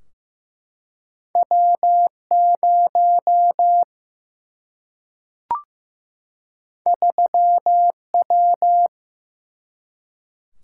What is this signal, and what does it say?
Morse code, heard as a keyed tone, about 16 wpm, 700 Hz. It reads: W0 3W
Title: Single-Letter Single-Number 15wpm